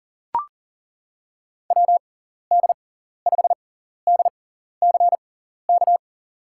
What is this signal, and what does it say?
Morse code, heard as a keyed tone, about 40 wpm, spaced as for 14 wpm, 700 Hz. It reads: WD5DCK
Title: USA Call Signs - 6 Characters - 6x Character Spacing - 40wpm